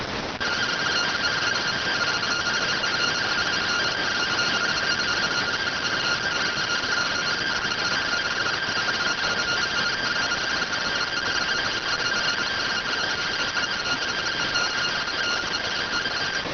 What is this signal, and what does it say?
Two-tone teleprinter signal, about 170 Hz shift.